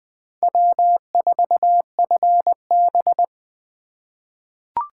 Morse code, heard as a keyed tone, about 20 wpm, 700 Hz. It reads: W4FB
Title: USA Call Signs - 4 Characters - 3x Character Spacing - 20wpm